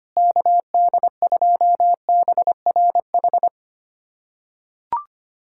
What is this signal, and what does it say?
Morse code, heard as a keyed tone, about 25 wpm, 700 Hz. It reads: KD2BRH
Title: USA Callsign DE Callsign - 6 Characters - 25wpm